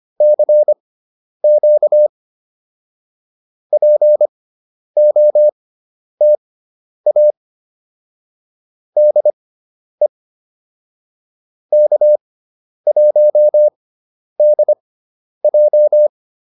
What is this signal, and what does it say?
Morse code, heard as a keyed tone, about 25 wpm, spaced as for 10 wpm, 600 Hz. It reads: CQ POTA DE K1DJ
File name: POTA202 10